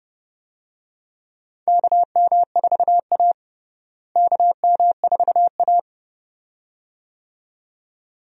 Morse code, heard as a keyed tone, about 30 wpm, 700 Hz. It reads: KM4A KM4A
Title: USA Call Signs - 4 Characters - Repeated 2x - 30wpm